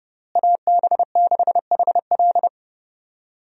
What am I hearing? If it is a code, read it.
Morse code, heard as a keyed tone, about 30 wpm, 700 Hz. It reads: AB6HL
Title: USA Call Signs - 5 Characters - Repeated 3x - 30wpm